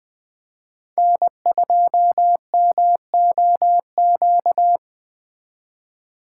Morse code, heard as a keyed tone, about 21 wpm, 700 Hz. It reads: N2MOQ
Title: USA Call Signs - 4x Character Spacing - 20wpm